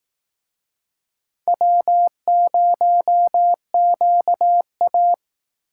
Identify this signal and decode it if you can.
Morse code, heard as a keyed tone, about 18 wpm, 700 Hz. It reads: W0QA
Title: Speed Racing - USA 1x2 Call Signs 15wpm